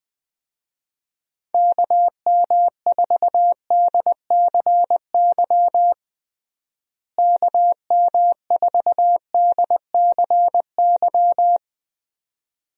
Morse code, heard as a keyed tone, about 21 wpm, 700 Hz. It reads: KM4DCY KM4DCY
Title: USA Call Signs - 6 Characters - Repeated 2x - 20wpm